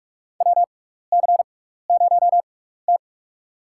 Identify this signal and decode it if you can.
Morse code, heard as a keyed tone, about 46 wpm, spaced as for 16 wpm, 700 Hz. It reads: WC0T
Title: USA Call Signs - 6x Character Spacing - 45wpm